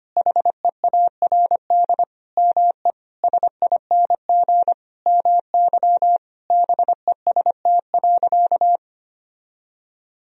Morse code, heard as a keyed tone, about 25 wpm, 700 Hz. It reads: HEARD ME SING MY BEST.
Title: Sentences from Top 1000 Words - Review Part 2 - 25wpm